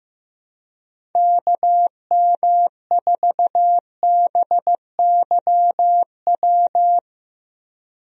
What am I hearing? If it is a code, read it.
Morse code, heard as a keyed tone, about 15 wpm, 700 Hz. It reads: KM4BYW
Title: USA Call Signs - 6 Characters - Repeated 3x - 15wpm